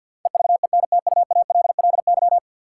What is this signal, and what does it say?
Morse code, heard as a keyed tone, about 53 wpm, 700 Hz. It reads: EVENTUALLY
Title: Sentences from Top 1000 Words - Review Part 6 - New Word Drill - 50wpm